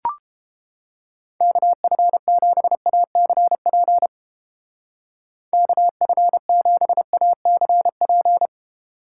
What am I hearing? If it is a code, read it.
Morse code, heard as a keyed tone, about 29 wpm, 700 Hz. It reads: KF7ACP KF7ACP